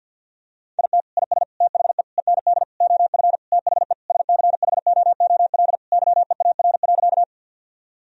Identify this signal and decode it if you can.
Morse code, heard as a keyed tone, about 52 wpm, 700 Hz. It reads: IT IS THE END OF THE SCHOOL YEAR.
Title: Sentences from Top 1000 Words - Review Part 1 - 50wpm